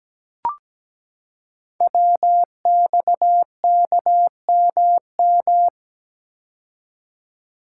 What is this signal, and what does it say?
Morse code, heard as a keyed tone, about 17 wpm, 700 Hz. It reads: WXKMM
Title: Code Groups - 5 Letters - 17wpm